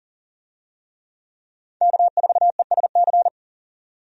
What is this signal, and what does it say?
Morse code, heard as a keyed tone, about 40 wpm, 700 Hz. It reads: K4ESC
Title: USA Call Signs - 5 Characters - 40wpm